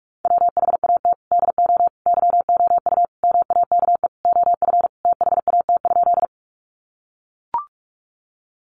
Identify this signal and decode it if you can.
Morse code, heard as a keyed tone, about 44 wpm, 700 Hz. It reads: WHAT DO YOU MAKE OF THAT?
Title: Sentences from Top 100 Words 45wpm